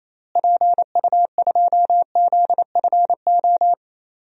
Morse code, heard as a keyed tone, about 28 wpm, 700 Hz. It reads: PU2ZFO
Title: DXCC-100 Call Signs - 6 Characters - Repeated 2x - 28wpm